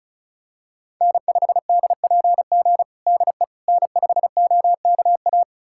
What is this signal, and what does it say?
Morse code, heard as a keyed tone, about 34 wpm, 700 Hz. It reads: N5DPG DE N5OKA
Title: USA Callsign DE Callsign - 5 Characters - 35wpm